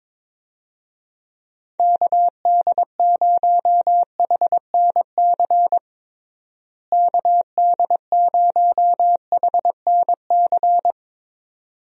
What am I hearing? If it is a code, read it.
Morse code, heard as a keyed tone, about 22 wpm, 700 Hz. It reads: KD0HNC KD0HNC
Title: USA Call Signs - 6 Characters - Repeated 2x - 22wpm